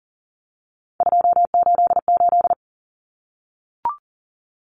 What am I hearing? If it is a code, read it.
Morse code, heard as a keyed tone, about 40 wpm, 700 Hz. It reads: 288
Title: Common Cut Numbers - 3-Digit 40wpm